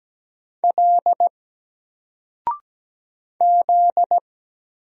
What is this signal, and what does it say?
Morse code, heard as a keyed tone, about 17 wpm, 700 Hz. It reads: L Z
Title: Single Letter-Number 17wpm